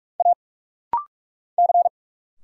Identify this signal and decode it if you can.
Morse code, heard as a keyed tone, about 46 wpm, 700 Hz. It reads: A C
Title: Single Letters - Rapid-Fire 45wpm